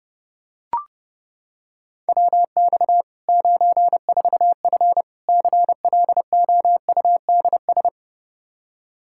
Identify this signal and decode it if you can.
Morse code, heard as a keyed tone, about 30 wpm, 700 Hz. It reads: WX 94F CLOUDS
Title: Practice QSO - Exchange 2 - Info - 30wpm